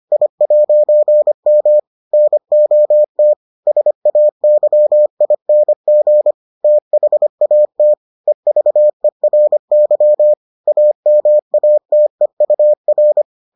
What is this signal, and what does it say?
Morse code, heard as a keyed tone, about 25 wpm, 600 Hz. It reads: I'M NOT SAYING THAT EVERY AMATEUR